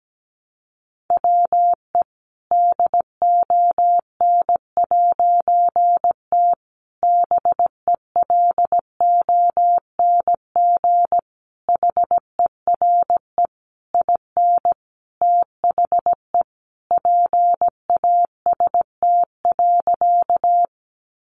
Morse code, heard as a keyed tone, about 17 wpm, 700 Hz. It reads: WE DON'T BELONG HERE IN THE PAST.
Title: A Sound Of Thunder 17wpm